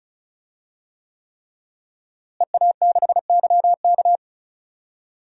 Morse code, heard as a keyed tone, about 34 wpm, 700 Hz. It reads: EA6YK
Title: DXCC-100 Call Signs - 5 Characters - Repeated 3x - 35wpm